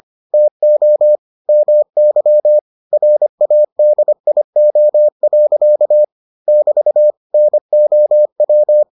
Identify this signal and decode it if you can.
Morse code, heard as a keyed tone, about 25 wpm, 600 Hz. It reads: TO MY RADIO. <BT> NOW